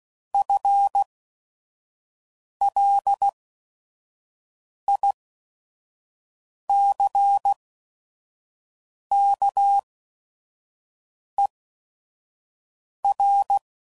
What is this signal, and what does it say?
Morse code, heard as a keyed tone, about 16 wpm, spaced as for 5 wpm, 800 Hz. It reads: FLICKER